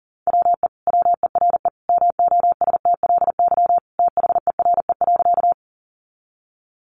Morse code, heard as a keyed tone, about 40 wpm, 700 Hz. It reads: WE WERE MOSTLY THERE.
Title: Sentences from Top 1300 Words 40wpm